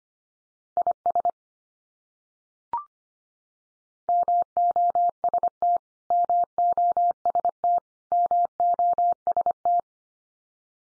Morse code, heard as a keyed tone, about 25 wpm, 700 Hz. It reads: IS MOST MOST MOST
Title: Top 100 Words 25wpm x3